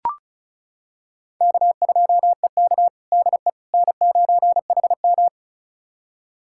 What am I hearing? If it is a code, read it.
Morse code, heard as a keyed tone, about 36 wpm, 700 Hz. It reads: K2EK DE N9HM
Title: USA Callsign DE Callsign - 4 Characters - 35wpm